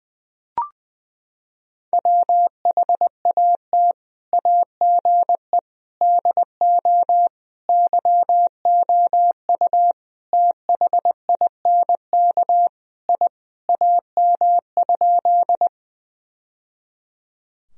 Morse code, heard as a keyed tone, about 20 wpm, 700 Hz. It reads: WHAT AGE DO YOU THINK I AM?
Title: Sentences from Top 1000 Words - Review Part 3 - 20wpm